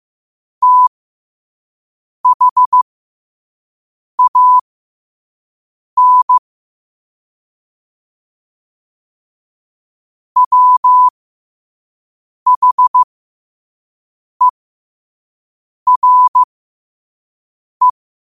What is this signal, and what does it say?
Morse code, heard as a keyed tone, about 15 wpm, spaced as for 5 wpm, 1000 Hz. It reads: THAN WHERE
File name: qotdfiles / 5wpm